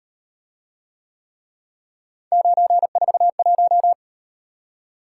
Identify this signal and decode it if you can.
Morse code, heard as a keyed tone, about 38 wpm, 700 Hz. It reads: 941